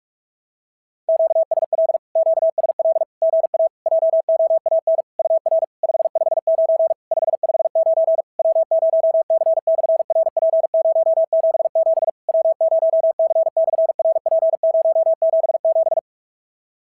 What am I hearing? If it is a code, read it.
Morse code, heard as a keyed tone, about 46 wpm, 650 Hz. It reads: QSL QSL GA JOAN UR 559 559 W0C/RP077 W0C/RP077
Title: SOTA QSO Practice 45wpm